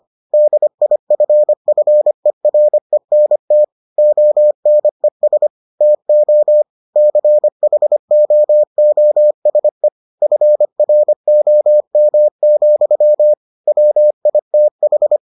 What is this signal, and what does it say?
Morse code, heard as a keyed tone, about 25 wpm, 600 Hz. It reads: DIFFERENT ONES TO CHOOSE FROM, WITH